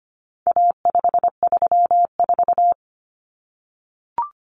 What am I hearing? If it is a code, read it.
Morse code, heard as a keyed tone, about 25 wpm, 700 Hz. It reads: A534
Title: Common Cut Numbers - 4-Digit 25wpm